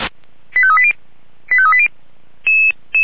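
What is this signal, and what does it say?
Five-tone selective call (ZVEI), heard as quick runs of five tones, tones of about 70 ms. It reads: 85380 85380
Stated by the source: ZVEI five-tone selective call 89200 received on an SDR.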